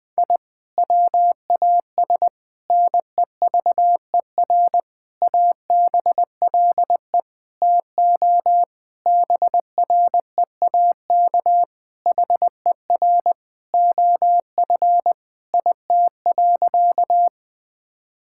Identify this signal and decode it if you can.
Morse code, heard as a keyed tone, about 20 wpm, 700 Hz. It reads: I WAS NEVER ABLE TO BREAK HER OF IT.
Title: Sentences from Top 1000 Words - Review Part 6 - 20wpm